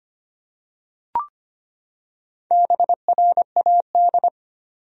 Morse code, heard as a keyed tone, about 25 wpm, 700 Hz. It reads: BRAD
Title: USA Common Names - Max 4 Letters - 25wpm